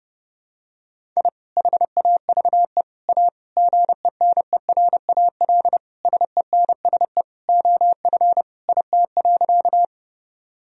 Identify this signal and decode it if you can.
Morse code, heard as a keyed tone, about 30 wpm, 700 Hz. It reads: I HAVE A GENERAL SENSE OF IT.
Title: Sentences from Top 1000 Words - Review Part 4 - 30wpm